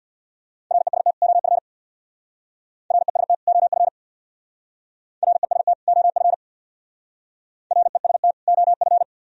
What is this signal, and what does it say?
Morse code, heard as a keyed tone, about 62 wpm, 700 Hz. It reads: REST OF REST OF REST OF REST OF
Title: Speed-Racing - Sets of 2 Words 50wpm